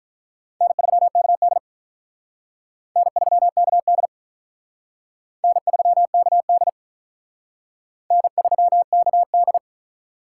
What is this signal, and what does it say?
Morse code, heard as a keyed tone, about 44 wpm, 700 Hz. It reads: N3KD N3KD N3KD N3KD